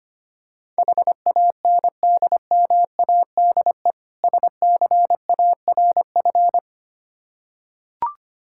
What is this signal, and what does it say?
Morse code, heard as a keyed tone, about 25 wpm, 700 Hz. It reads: HANDMADE SCARF
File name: practice / 25wpm / Difficult Words with Hints - 25wpm